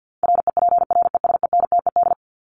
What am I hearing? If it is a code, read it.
Morse code, heard as a keyed tone, about 50 wpm, 700 Hz. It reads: REPRESENTED
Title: ED Suffix - Top 100 Words with Base Words 50wpm